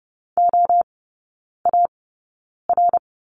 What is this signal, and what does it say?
Morse code, heard as a keyed tone, about 30 wpm, spaced as for 9 wpm, 700 Hz. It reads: OAR
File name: 081 Lesson 14e - Number 2 - Three Characters 30wpm